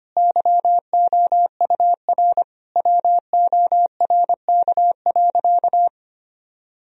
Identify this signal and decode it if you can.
Morse code, heard as a keyed tone, about 25 wpm, 700 Hz. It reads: YOUR WORK.
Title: Sentences from Top 1000 Words - Review Part 4 - 25wpm